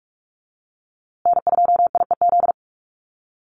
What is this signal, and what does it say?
Morse code, heard as a keyed tone, about 44 wpm, 700 Hz. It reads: N2IEZ
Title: USA Call Signs Repeated 3x 45wpm